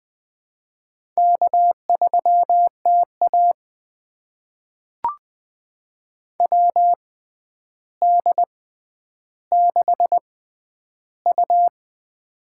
Morse code, heard as a keyed tone, about 20 wpm, spaced as for 10 wpm, 700 Hz. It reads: K3TA WD6U
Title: USA Call Signs - 4 Characters - 6x Character Spacing - 20wpm